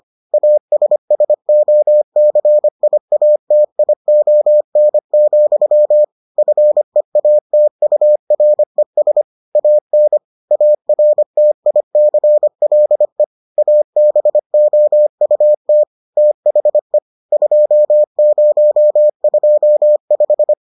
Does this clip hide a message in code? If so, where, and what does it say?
Yes: Morse code from 0.3 to 20.5 s, heard as a keyed tone, about 25 wpm, 600 Hz. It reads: ASSOCIATION, FEATURES AN ARTICLE ABOUT THE 2025